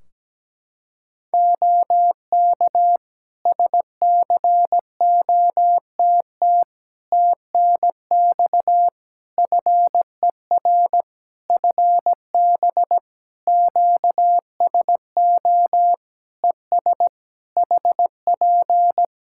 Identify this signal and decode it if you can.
Morse code, heard as a keyed tone, about 17 wpm, 700 Hz. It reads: OK SCOTT TNX FER FB QSO ES HP
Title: Practice QSO - Exchange 4 - Ending - 17wpm